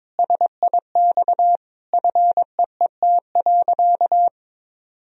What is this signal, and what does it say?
Morse code, heard as a keyed tone, about 22 wpm, 700 Hz. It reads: SIX FEET.
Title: Sentences from Top 1000 Words - Review Part 2 - 22wpm